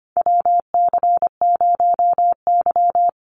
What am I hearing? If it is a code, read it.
Morse code, heard as a keyed tone, about 25 wpm, 700 Hz. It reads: WC0Y
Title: NA SOTA Chaser Callsigns 25wpm